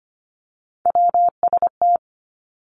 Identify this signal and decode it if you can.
Morse code, heard as a keyed tone, about 25 wpm, 700 Hz. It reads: WST
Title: Three Challenging Letter-Number Combinations 25wpm